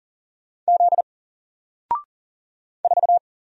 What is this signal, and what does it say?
Morse code, heard as a keyed tone, about 40 wpm, 700 Hz. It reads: Z 4
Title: Single Letter-Number Alt 40wpm